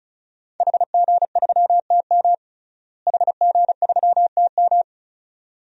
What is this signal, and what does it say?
Morse code, heard as a keyed tone, about 34 wpm, 700 Hz. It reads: HG3TM HG3TM
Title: DXCC-100 Call Signs - 5 Characters - Repeated 2x - 35wpm